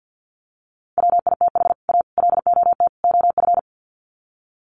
Morse code, heard as a keyed tone, about 50 wpm, 700 Hz. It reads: WITH A LOT OF
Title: Sets of 4 Words 50wpm